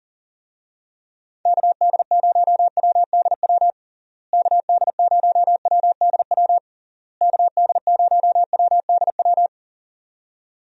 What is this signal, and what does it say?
Morse code, heard as a keyed tone, about 40 wpm, 700 Hz. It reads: KD0WDW KD0WDW KD0WDW
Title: USA Call Signs - 6 Characters - Repeated 3x - 40wpm